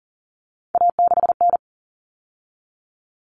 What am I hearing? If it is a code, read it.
Morse code, heard as a keyed tone, about 40 wpm, 700 Hz. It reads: A6N
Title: All Cut Numbers - 3-Digit 40wpm